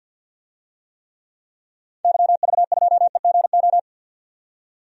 Morse code, heard as a keyed tone, about 46 wpm, 700 Hz. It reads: YV2EGO
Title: DXCC-100 Call Signs - 6 Characters - 50wpm